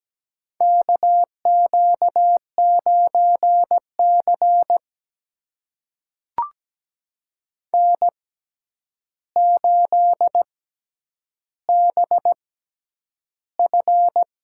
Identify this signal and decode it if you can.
Morse code, heard as a keyed tone, about 17 wpm, spaced as for 9 wpm, 700 Hz. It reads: KQ9C N8BF
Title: USA Call Signs - 4 Characters - 6x Character Spacing - 17wpm